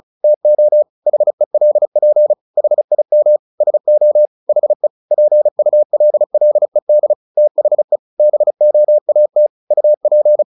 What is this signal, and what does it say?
Morse code, heard as a keyed tone, about 34 wpm, 600 Hz. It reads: TO HELP HIM SO HE PULLED THE BOAT UP
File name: the old man and the sea 2 35wpm